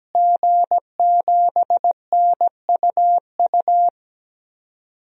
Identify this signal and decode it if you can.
Morse code, heard as a keyed tone, about 17 wpm, 700 Hz. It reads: G7NUU
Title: DXCC-100 Call Signs - 5 Characters - Repeated 2x - 17wpm